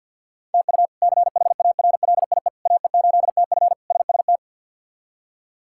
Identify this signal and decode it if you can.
Morse code, heard as a keyed tone, about 50 wpm, 700 Hz. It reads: TU CHARLIE AE8TF SST
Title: K1USN SST QSO Practice - 50wpm